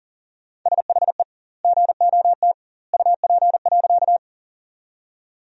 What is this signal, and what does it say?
Morse code, heard as a keyed tone, about 37 wpm, 700 Hz. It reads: SHE GOT UP.
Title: 2x Word-Spacing - Sentences from Top 300 Words - 40wpm